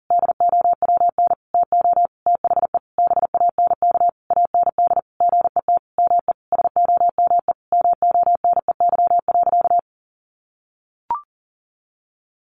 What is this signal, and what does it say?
Morse code, heard as a keyed tone, about 40 wpm, 700 Hz. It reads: DOWN TO THE BANK AND GET ME SOME MONEY.
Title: Sentences from Top 1500 Words 40wpm